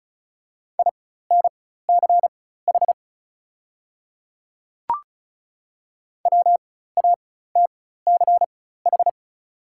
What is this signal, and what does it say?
Morse code, heard as a keyed tone, about 34 wpm, spaced as for 16 wpm, 700 Hz. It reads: INCH WATCH
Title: 4x Character Spacing - Top 500 Words - 35wpm